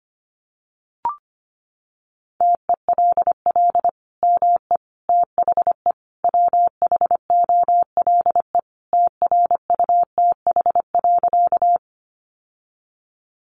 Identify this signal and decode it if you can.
Morse code, heard as a keyed tone, about 25 wpm, 700 Hz. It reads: TELL ME THE WHOLE TRUTH.
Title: Sentences from Top 800 Words 25wpm